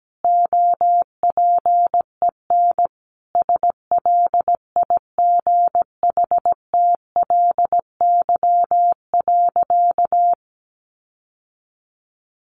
Morse code, heard as a keyed tone, about 17 wpm, 700 Hz. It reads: OPEN SLIGHTLY.
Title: Sentences from Top 1300 Words 17wpm